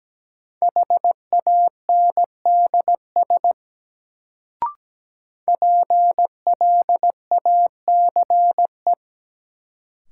Morse code, heard as a keyed tone, about 17 wpm, 700 Hz. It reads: HANDS PLACE
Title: Sets of 4 Words - New Word Drill - 17wpm